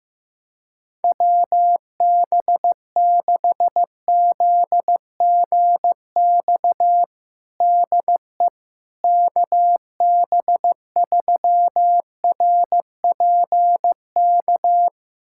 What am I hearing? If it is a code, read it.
Morse code, heard as a keyed tone, about 15 wpm, 700 Hz. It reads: WB6ZGX DE KB3RPK
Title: USA Callsign DE Callsign 15wpm